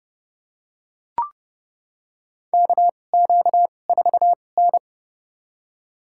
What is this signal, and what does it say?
Morse code, heard as a keyed tone, about 30 wpm, spaced as for 22 wpm, 700 Hz. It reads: KQ4N
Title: USA Call Signs - 4 Characters - 2x Character Spacing - 30wpm